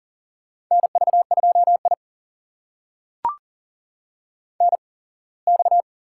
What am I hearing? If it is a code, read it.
Morse code, heard as a keyed tone, about 40 wpm, spaced as for 24 wpm, 700 Hz. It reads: NV2I NX
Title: USA Call Signs - 4 Characters - 8x Character Spacing - 40wpm